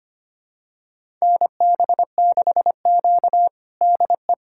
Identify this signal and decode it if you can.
Morse code, heard as a keyed tone, about 25 wpm, 700 Hz. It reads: NB6Q DE
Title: USA Callsign DE Callsign - 4 Characters - 25wpm